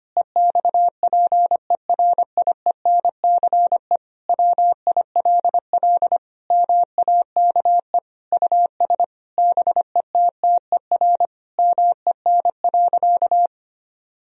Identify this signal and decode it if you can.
Morse code, heard as a keyed tone, about 25 wpm, 700 Hz. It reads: EXPERIENCE WILL MAKE US BETTER MEN.
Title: Sentences from Top 1000 Words - Review Part 3 - 25wpm